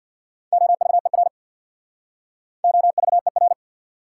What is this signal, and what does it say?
Morse code, heard as a keyed tone, about 67 wpm, 700 Hz. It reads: OVER OVER